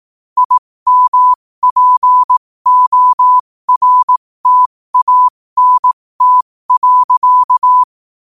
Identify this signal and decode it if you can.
Morse code, heard as a keyed tone, about 18 wpm, spaced as for 15 wpm, 1000 Hz. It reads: IMPORTANT.
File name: qotdfiles / 15wpm